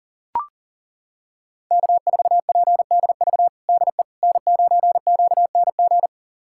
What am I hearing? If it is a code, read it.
Morse code, heard as a keyed tone, about 37 wpm, 700 Hz. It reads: K4PDV DE N9QNG
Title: USA Callsign DE Callsign 40wpm